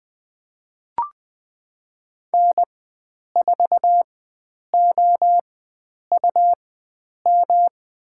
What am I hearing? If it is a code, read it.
Morse code, heard as a keyed tone, about 20 wpm, spaced as for 9 wpm, 700 Hz. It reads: N4OUM
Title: USA Call Signs - 4x Character Spacing - 20wpm